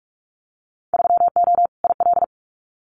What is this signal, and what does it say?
Morse code, heard as a keyed tone, about 44 wpm, 700 Hz. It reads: 3O IL
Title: Winter Field Day Exchanges Encore 45wpm